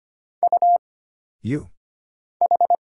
Morse code, heard as a keyed tone, about 25 wpm, 700 Hz. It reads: U H
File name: Single Character (Letter:Number:Punctuation:Prosign) - Mind-Melt - 25wpm